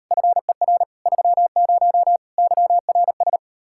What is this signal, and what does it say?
Morse code, heard as a keyed tone, about 36 wpm, 700 Hz. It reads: FER 30 YRS